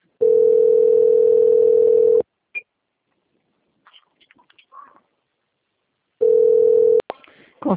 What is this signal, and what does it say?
Ringback tone (North America), 440 + 480 Hz.
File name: out- - - - - .10723